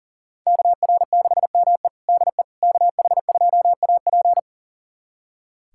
Morse code, heard as a keyed tone, about 37 wpm, 700 Hz. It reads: KR6ME DE KH2AP
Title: USA Callsign DE Callsign - 5 Characters - 40wpm